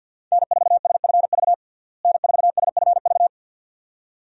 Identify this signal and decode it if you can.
Morse code, heard as a keyed tone, about 52 wpm, 700 Hz. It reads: N4SFV N4SFV
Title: USA Call Signs - 5 Characters - Repeated 2x - 50wpm